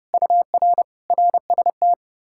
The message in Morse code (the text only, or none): UR RST